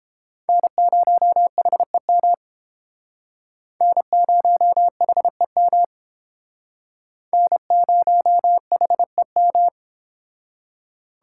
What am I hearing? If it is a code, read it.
Morse code, heard as a keyed tone, about 29 wpm, 700 Hz. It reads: N0HEM N0HEM N0HEM